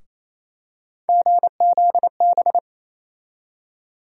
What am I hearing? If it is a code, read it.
Morse code, heard as a keyed tone, about 28 wpm, 700 Hz. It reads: GZB
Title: USA Call Sign Postfixes - 3 Letters 28wpm